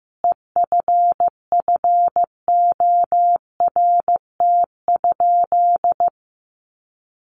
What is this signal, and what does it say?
Morse code, heard as a keyed tone, about 15 wpm, 700 Hz. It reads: EFFORT?
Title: Sentences from Top 1200 Words 15wpm